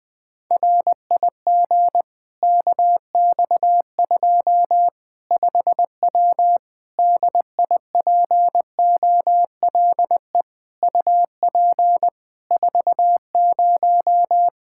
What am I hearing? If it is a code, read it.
Morse code, heard as a keyed tone, about 21 wpm, 700 Hz. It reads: RIG KX2 5W DIPOLE UP 40
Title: Practice QSO - Exchange 2 - Info - 20wpm